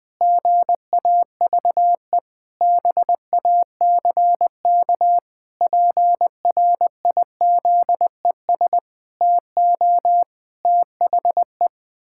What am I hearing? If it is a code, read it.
Morse code, heard as a keyed tone, about 20 wpm, 700 Hz. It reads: GAVE BACK PRIZES TO THE
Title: Pangram Challenge v2 - 20wpm